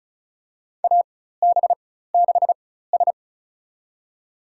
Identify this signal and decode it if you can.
Morse code, heard as a keyed tone, about 34 wpm, spaced as for 16 wpm, 700 Hz. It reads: AB6S
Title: USA Call Signs - 4 Characters - 4x Character Spacing - 35wpm